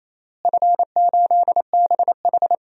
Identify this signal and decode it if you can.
Morse code, heard as a keyed tone, about 27 wpm, 700 Hz. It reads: F8BH
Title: DXCC-100 Call Signs - 4 Characters - Repeated 3x - 28wpm